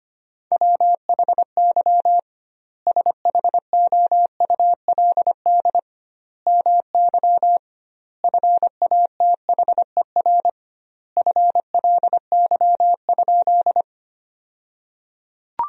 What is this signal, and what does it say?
Morse code, heard as a keyed tone, about 25 wpm, 700 Hz. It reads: WHY SHOULD MY FATHER FLY?
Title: 2x Word-Spacing - Sentences from Top 500 Words - 25wpm